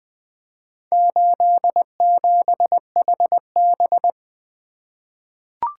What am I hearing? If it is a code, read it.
Morse code, heard as a keyed tone, about 20 wpm, 700 Hz. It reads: 87HB
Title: Four Letter-Number Combinations - 20wpm